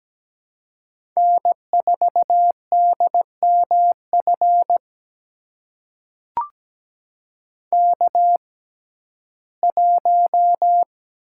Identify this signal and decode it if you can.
Morse code, heard as a keyed tone, about 17 wpm, spaced as for 12 wpm, 700 Hz. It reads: N4DMF K1
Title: USA Call Signs - 5 Characters - 6x Character Spacing - 17wpm